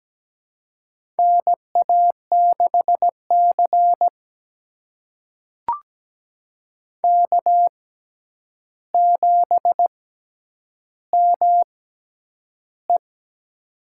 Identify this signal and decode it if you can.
Morse code, heard as a keyed tone, about 17 wpm, spaced as for 9 wpm, 700 Hz. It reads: NA6C K7ME
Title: USA Call Signs - 4 Characters - 6x Character Spacing - 17wpm